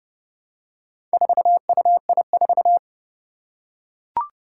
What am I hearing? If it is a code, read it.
Morse code, heard as a keyed tone, about 29 wpm, 700 Hz. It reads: 4UI4
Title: Four Challenging Letter-Number Combinations 30wpm